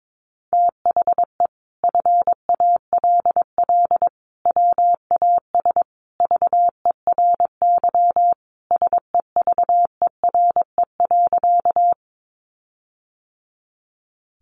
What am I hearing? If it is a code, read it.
Morse code, heard as a keyed tone, about 22 wpm, 700 Hz. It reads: THE FALL WAS VERY SEVERE.
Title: Sentences from Top 1400 Words 22wpm